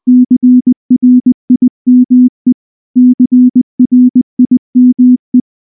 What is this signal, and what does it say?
Morse code, heard as a keyed tone, about 19 wpm, 250 Hz. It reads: CRIME CRIME
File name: morse / morse-training-20WPM-3